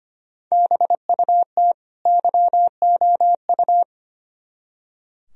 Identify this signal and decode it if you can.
Morse code, heard as a keyed tone, about 33 wpm, 700 Hz. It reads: BUT YOU